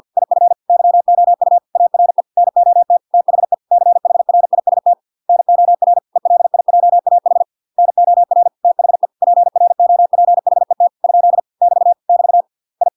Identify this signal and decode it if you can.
Morse code, heard as a keyed tone, about 50 wpm, 700 Hz. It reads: IF YOU ARE NOT THE CHRIST NOR ELIJAH NOR THE PROPHET ? <BT> <BT> I